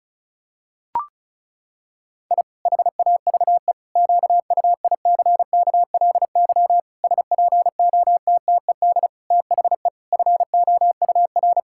Morse code, heard as a keyed tone, about 36 wpm, 700 Hz. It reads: I HAVE QUICKLY SPOTTED THE FOUR
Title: Pangram Challenge v2 - 35wpm